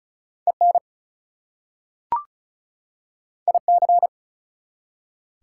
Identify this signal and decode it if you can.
Morse code, heard as a keyed tone, about 36 wpm, 700 Hz. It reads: EN IC
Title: Prefixes and Suffixes 35wpm